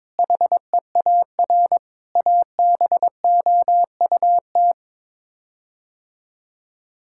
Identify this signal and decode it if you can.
Morse code, heard as a keyed tone, about 22 wpm, 700 Hz. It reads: HEAR ABOUT
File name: practice / 22wpm / Sets of 6 Words - 22wpm